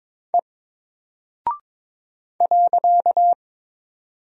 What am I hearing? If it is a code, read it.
Morse code, heard as a keyed tone, about 22 wpm, 700 Hz. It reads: E .
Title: Single Character (Letter:Number:Punctuation:Prosign) Alt 22wpm